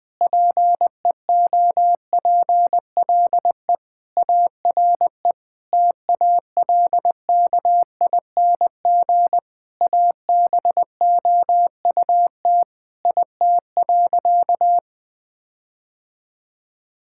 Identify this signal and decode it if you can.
Morse code, heard as a keyed tone, about 20 wpm, 700 Hz. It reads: PEOPLE ARE TALKING ABOUT IT.
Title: Sentences with 5-Word Phrases - Spoken Beginning - 20wpm